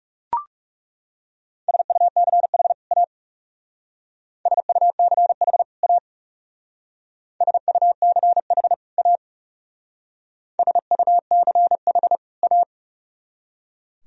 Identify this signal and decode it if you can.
Morse code, heard as a keyed tone, about 38 wpm, 700 Hz. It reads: SUCH A SUCH A SUCH A SUCH A